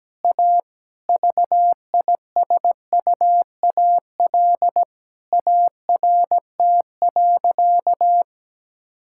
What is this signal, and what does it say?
Morse code, heard as a keyed tone, about 17 wpm, 700 Hz. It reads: A VISUAL ART.
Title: Sentences from Top 1000 Words - Review Part 6 - 17wpm